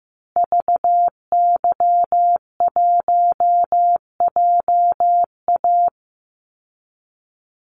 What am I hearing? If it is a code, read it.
Morse code, heard as a keyed tone, about 15 wpm, 700 Hz. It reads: VY1JA
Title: Call Signs Repeated 3x - 15wpm